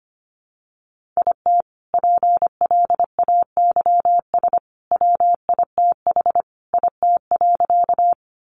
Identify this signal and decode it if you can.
Morse code, heard as a keyed tone, about 25 wpm, 700 Hz. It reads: IT PLAYS WITH IT.
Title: Additional Easy Phrases 25wpm